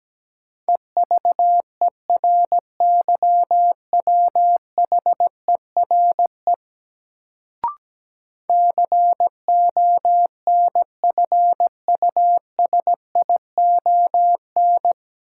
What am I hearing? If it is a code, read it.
Morse code, heard as a keyed tone, about 17 wpm, 700 Hz. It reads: EVERYWHERE CONFUSION
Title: Sentences from Top 1600 Words - New Word Drill - 17wpm